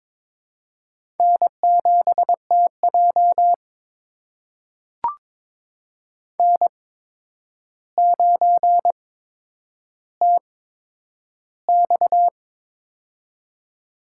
Morse code, heard as a keyed tone, about 22 wpm, spaced as for 9 wpm, 700 Hz. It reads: N7TJ N9TX
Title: USA Call Signs - 4 Characters - 8x Character Spacing - 22wpm